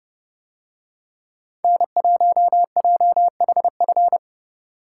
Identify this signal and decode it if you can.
Morse code, heard as a keyed tone, about 29 wpm, 700 Hz. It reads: N1JHF